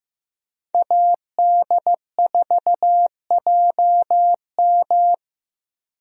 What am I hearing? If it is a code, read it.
Morse code, heard as a keyed tone, about 15 wpm, 700 Hz. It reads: AD4JM
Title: USA Call Signs - 5 Characters - Repeated 3x - 15wpm